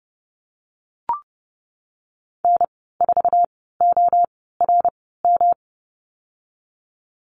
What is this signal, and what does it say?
Morse code, heard as a keyed tone, about 32 wpm, spaced as for 17 wpm, 700 Hz. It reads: N4ORM
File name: Lesson 19g - Letter w - Call Signs 30wpm